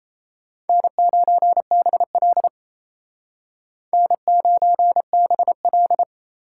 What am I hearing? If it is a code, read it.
Morse code, heard as a keyed tone, about 35 wpm, 700 Hz. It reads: N9BL N9BL
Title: Speed Racing - USA 1x2 Call Signs 28wpm